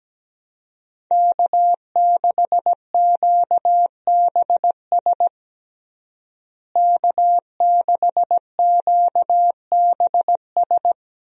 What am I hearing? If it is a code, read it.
Morse code, heard as a keyed tone, about 17 wpm, 700 Hz. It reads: K6QBS K6QBS
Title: USA Call Signs - 5 Characters - Repeated 2x - 17wpm